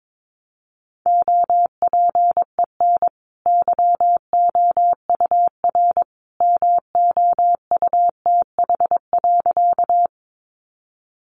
Morse code, heard as a keyed tone, about 22 wpm, 700 Hz. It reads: OPEN YOUR MOUTH.
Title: Sentences from Top 1500 Words 22wpm